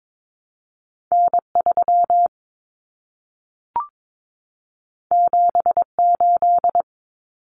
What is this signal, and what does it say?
Morse code, heard as a keyed tone, about 22 wpm, 700 Hz. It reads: N3 78
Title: Common Cut Numbers - 2-Digit 22wpm